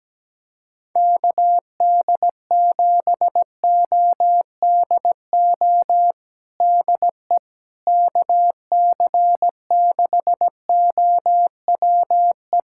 Morse code, heard as a keyed tone, about 17 wpm, 700 Hz. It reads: KD7ODO DE KC6OWE
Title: USA Callsign DE Callsign - 6 Characters - 17wpm